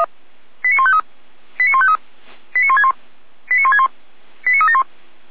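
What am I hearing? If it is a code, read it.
Five-tone selective call (ZVEI), heard as quick runs of five tones, tones of about 70 ms. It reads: 89153 89153 89161 89161 89371
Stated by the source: ZVEI five-tone selective call received by SDR.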